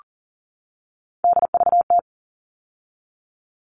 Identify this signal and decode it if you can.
Morse code, heard as a keyed tone, about 40 wpm, 700 Hz. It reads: DVT
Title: All Cut Numbers - 3-Digit 40wpm